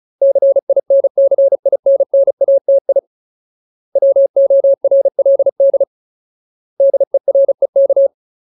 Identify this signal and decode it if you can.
Morse code, heard as a keyed tone, about 34 wpm, 550 Hz. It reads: CINCINNATI WORLD DEREK
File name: Ultimate Word List / 35WPM Ultimate